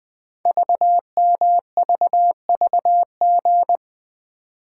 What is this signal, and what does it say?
Morse code, heard as a keyed tone, about 20 wpm, 700 Hz. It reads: VMVVG
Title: Code Groups - 5 Letters - 20wpm